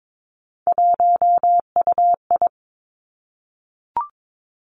Morse code, heard as a keyed tone, about 22 wpm, 700 Hz. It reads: 1UI
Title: Three Challenging Letter-Number Combinations 22wpm